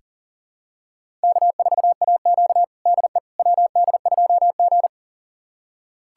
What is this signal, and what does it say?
Morse code, heard as a keyed tone, about 40 wpm, 700 Hz. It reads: K4AQ DE WD2G
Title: USA Callsign DE Callsign - 4 Characters - 40wpm